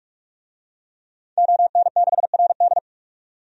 Morse code, heard as a keyed tone, about 46 wpm, 700 Hz. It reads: ON6RD
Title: DXCC-100 Call Signs - 5 Characters - 45wpm